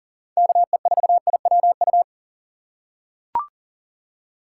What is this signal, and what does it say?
Morse code, heard as a keyed tone, about 40 wpm, 700 Hz. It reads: KE4IWU
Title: USA Call Signs - 6 Characters - 4x Character Spacing - 40wpm